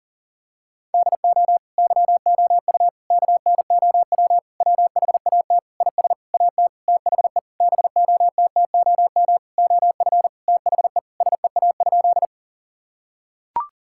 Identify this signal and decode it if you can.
Morse code, heard as a keyed tone, about 40 wpm, 700 Hz. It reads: DO YOU KNOW WHAT IS AT THE BOTTOM OF THE SEA?
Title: Sentences from Top 1000 Words - Review Part 6 - 40wpm